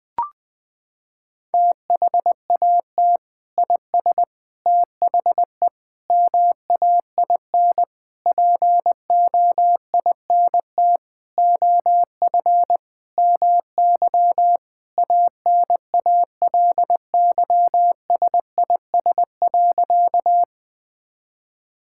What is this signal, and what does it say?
Morse code, heard as a keyed tone, about 20 wpm, 700 Hz. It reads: THAT IS THE MAIN POINT OF MY ANALYSIS.
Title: Sentences from Top 1000 Words - Review Part 5 - 20wpm